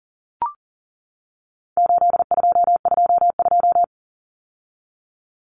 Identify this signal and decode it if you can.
Morse code, heard as a keyed tone, about 40 wpm, 700 Hz. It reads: 8222
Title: Common Cut Numbers - 4-Digit 40wpm